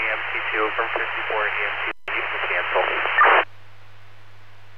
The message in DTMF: none

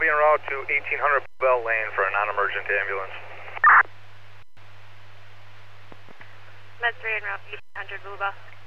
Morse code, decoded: none

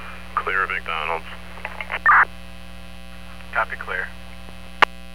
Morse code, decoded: none

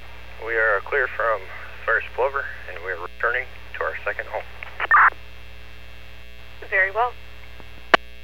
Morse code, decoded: none

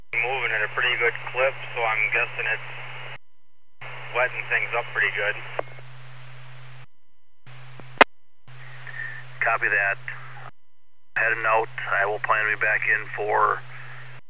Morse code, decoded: none

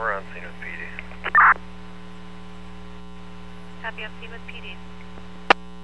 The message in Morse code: none

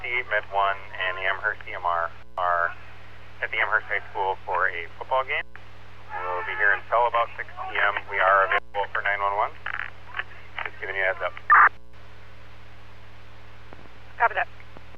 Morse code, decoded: none

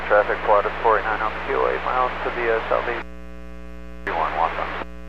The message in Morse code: none

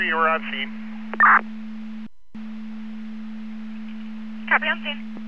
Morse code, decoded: none

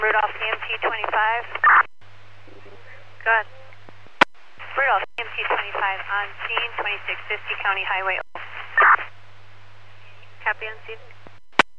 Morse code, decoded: none

